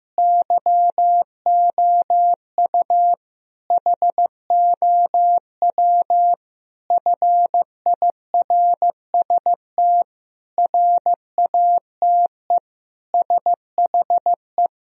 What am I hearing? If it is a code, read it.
Morse code, heard as a keyed tone, about 15 wpm, 700 Hz. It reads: YOU HOW FIRST RATE SHE
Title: Sentences from Top 1000 Words - Review Part 4 - 15wpm